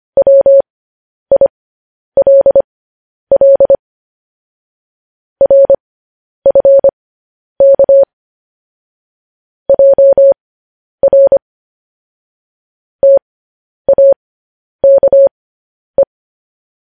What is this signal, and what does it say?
Morse code, heard as a keyed tone, about 25 wpm, spaced as for 10 wpm, 550 Hz. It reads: WILL RFK JR TAKE